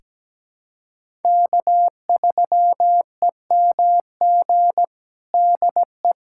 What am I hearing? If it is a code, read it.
Morse code, heard as a keyed tone, about 17 wpm, 700 Hz. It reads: K3EMG DE
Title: USA Callsign DE Callsign 17wpm